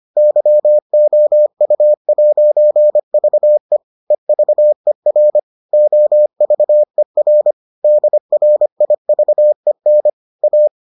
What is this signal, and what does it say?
Morse code, heard as a keyed tone, about 25 wpm, 600 Hz. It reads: YOU'VE EVER OVER DRIVEN A